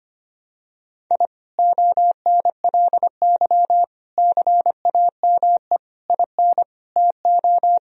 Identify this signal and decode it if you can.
Morse code, heard as a keyed tone, about 25 wpm, 700 Hz. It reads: I ONLY CAME IN TO
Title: Sentences from Top 1000 Words - Review Part 2 - 25wpm